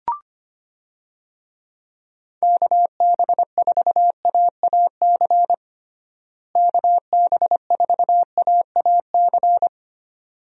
Morse code, heard as a keyed tone, about 25 wpm, 700 Hz. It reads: KB4AAC KB4AAC
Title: USA Call Signs - 6 Characters - Repeated 2x - 25wpm